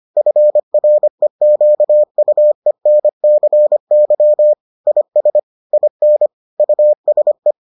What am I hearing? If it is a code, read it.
Morse code, heard as a keyed tone, about 25 wpm, 600 Hz. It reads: FREQUENCY IS IN USE